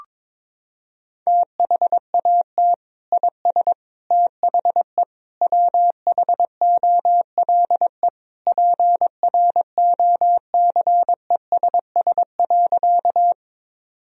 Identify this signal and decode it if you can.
Morse code, heard as a keyed tone, about 22 wpm, 700 Hz. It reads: THAT IS THE WHOLE PROCESS.
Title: Sentences from Top 1000 Words - Review Part 3 - 22wpm